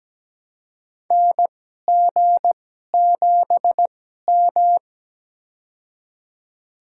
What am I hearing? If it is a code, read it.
Morse code, heard as a keyed tone, about 17 wpm, spaced as for 12 wpm, 700 Hz. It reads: NG7M
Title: USA Call Signs - 4 Characters - 2x Character Spacing - 17wpm